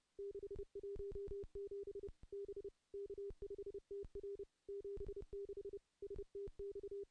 Morse code, heard as a keyed tone, about 30 wpm, 400 Hz. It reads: 617 B K5TR 76 STX
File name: contests / CW-SS-k5tr- -0400